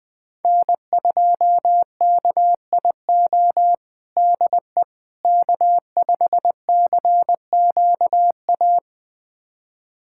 Morse code, heard as a keyed tone, about 21 wpm, 700 Hz. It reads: N2KIO DE K5CQA
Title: USA Callsign DE Callsign - 5 Characters - 20wpm